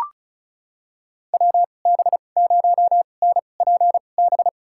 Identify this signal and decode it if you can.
Morse code, heard as a keyed tone, about 34 wpm, spaced as for 25 wpm, 700 Hz. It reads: WB0NPB
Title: USA Call Signs - 2x Character Spacing - 35wpm